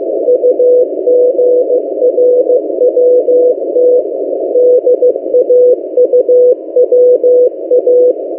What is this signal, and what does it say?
Morse code, heard as a keyed tone, about 15 wpm, 500 Hz. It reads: UGRWT DAUWA
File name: M08a- -1355utc-8096khz